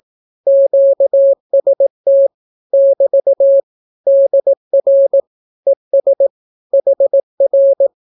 Morse code, heard as a keyed tone, about 18 wpm, 550 Hz. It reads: QST <BT> DR ES HR
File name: naukapodstaw / skroty / skroty